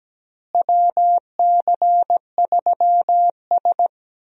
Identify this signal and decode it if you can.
Morse code, heard as a keyed tone, about 17 wpm, 700 Hz. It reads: WC3S
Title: USA Call Signs - 4 Characters - Repeated 3x - 17wpm